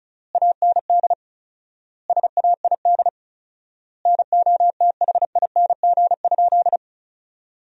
Morse code, heard as a keyed tone, about 34 wpm, 700 Hz. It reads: AND SAID NOTHING?
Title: 4x Word-Spacing - Sentences from Top 500 Words - 35wpm